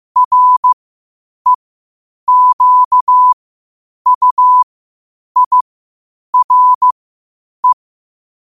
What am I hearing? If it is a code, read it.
Morse code, heard as a keyed tone, about 15 wpm, spaced as for 8 wpm, 1000 Hz. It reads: REQUIRE
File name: qotdfiles / 8wpm